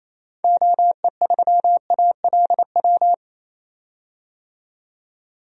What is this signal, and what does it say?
Morse code, heard as a keyed tone, about 28 wpm, 700 Hz. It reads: OE3ALW
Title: DXCC-100 Call Signs - 6 Characters - 28wpm